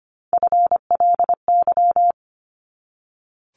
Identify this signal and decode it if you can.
Morse code, heard as a keyed tone, about 25 wpm, 700 Hz. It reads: FLY
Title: Top 500 Words 25wpm